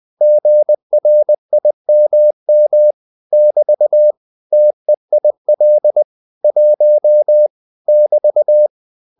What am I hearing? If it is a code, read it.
Morse code, heard as a keyed tone, about 20 wpm, 600 Hz. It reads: GRIMM <BT> TEIL 1 <BT>